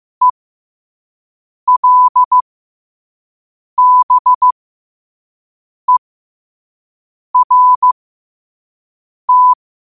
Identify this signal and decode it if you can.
Morse code, heard as a keyed tone, about 15 wpm, spaced as for 5 wpm, 1000 Hz. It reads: ELBERT